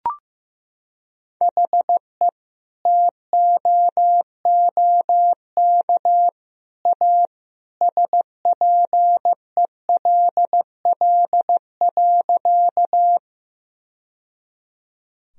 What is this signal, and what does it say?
Morse code, heard as a keyed tone, about 15 wpm, 700 Hz. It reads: HE TOOK A SPELL.
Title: Sentences from Top 1000 Words - Review Part 1 - 15wpm